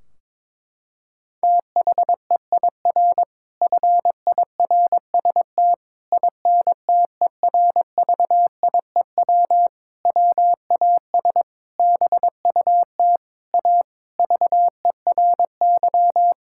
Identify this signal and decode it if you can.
Morse code, heard as a keyed tone, about 22 wpm, 700 Hz. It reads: THEIR FIRST INTERVIEW WAS BUT A VERY
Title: Sentences from Top 1600 Words 22wpm